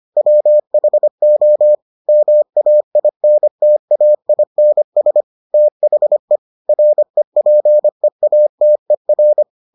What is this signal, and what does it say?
Morse code, heard as a keyed tone, about 25 wpm, 600 Hz. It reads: WHO MAINTAINS THE REPEATER